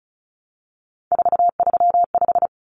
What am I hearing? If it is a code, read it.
Morse code, heard as a keyed tone, about 34 wpm, 700 Hz. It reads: 435
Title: Common Cut Numbers - 3-Digit 35wpm